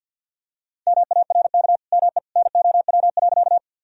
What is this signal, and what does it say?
Morse code, heard as a keyed tone, about 47 wpm, 700 Hz. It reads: MARK ME NOW.
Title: Sentences from Top 1000 Words - Review Part 1 - 50wpm